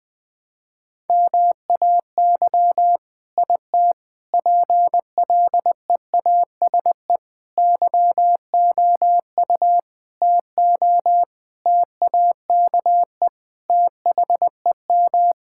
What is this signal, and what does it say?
Morse code, heard as a keyed tone, about 21 wpm, 700 Hz. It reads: MAY IT PLEASE YOU TO TAKE THEM
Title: Sentences from Top 1600 Words 20wpm